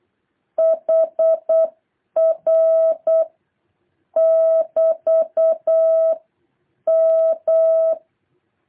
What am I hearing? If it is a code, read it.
Morse code, heard as a keyed tone, about 8 wpm, 650 Hz. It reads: HR <BT> M